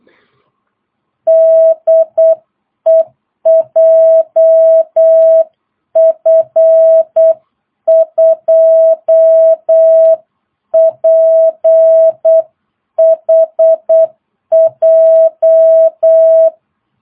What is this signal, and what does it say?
Morse code, heard as a keyed tone, about 8 wpm, 660 Hz. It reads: DE JF2PHJ
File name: jh8cbh / 2022cw